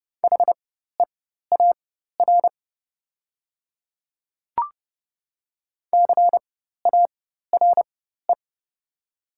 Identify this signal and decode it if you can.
Morse code, heard as a keyed tone, about 30 wpm, spaced as for 14 wpm, 700 Hz. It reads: HEAR CARE
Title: 4x Character Spacing - Top 500 Words - 30wpm